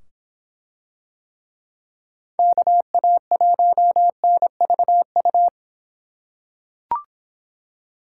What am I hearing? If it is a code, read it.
Morse code, heard as a keyed tone, about 24 wpm, 700 Hz. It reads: KA1NVU